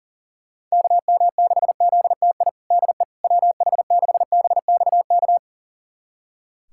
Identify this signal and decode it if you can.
Morse code, heard as a keyed tone, about 37 wpm, 700 Hz. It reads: KM6ZTI DE WH6BXK
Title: USA Callsign DE Callsign - 6 Characters - 40wpm